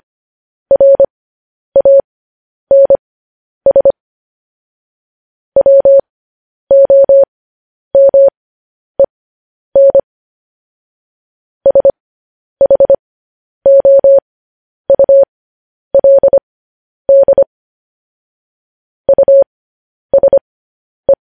Morse code, heard as a keyed tone, about 25 wpm, spaced as for 10 wpm, 550 Hz. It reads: RANS WOMEN SHOULD USE
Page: News Headlines in Morse Code at 10 WPM